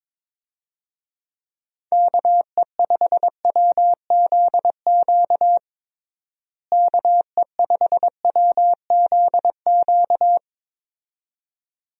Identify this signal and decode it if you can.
Morse code, heard as a keyed tone, about 22 wpm, 700 Hz. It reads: KE5WZQ KE5WZQ
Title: USA Call Signs - 6 Characters - Repeated 2x - 22wpm